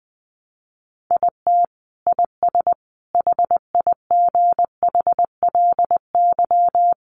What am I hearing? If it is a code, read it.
Morse code, heard as a keyed tone, about 20 wpm, 700 Hz. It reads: IT IS HIGHLY
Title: Sentences from Top 1100 Words 20wpm